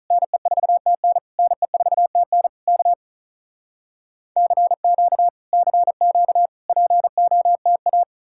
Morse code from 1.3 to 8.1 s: NE4TN K CQ CQ POTA

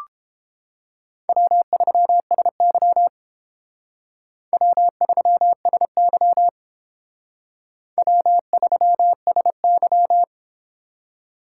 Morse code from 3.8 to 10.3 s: W3SY W3SY

